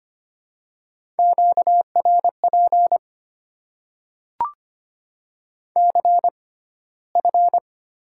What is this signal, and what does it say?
Morse code, heard as a keyed tone, about 25 wpm, spaced as for 15 wpm, 690 Hz. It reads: QRP CF
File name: Top 90 QSO Elements - 6x Character Spacing - 25wpm